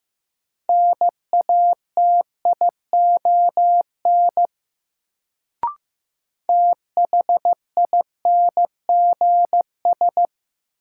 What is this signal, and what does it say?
Morse code, heard as a keyed tone, about 15 wpm, 700 Hz. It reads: NATION THINGS
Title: Sets of 3 Words - New Word Drill - 15wpm